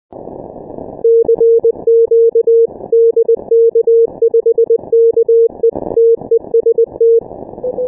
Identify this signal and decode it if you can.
Morse code, heard as a keyed tone, about 20 wpm, 460 Hz. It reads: CQ DK5KE TEST
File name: dk5ke / Contest2QSOs 100BpM